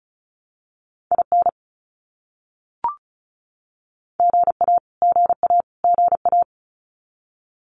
Morse code, heard as a keyed tone, about 34 wpm, 700 Hz. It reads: IN GA GA GA
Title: US States Repeated 3x 35wpm